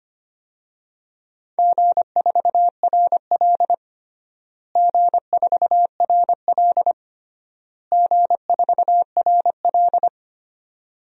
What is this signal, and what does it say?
Morse code, heard as a keyed tone, about 25 wpm, 700 Hz. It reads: G4RL G4RL G4RL
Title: DXCC-100 Call Signs - 4 Characters - Repeated 3x - 25wpm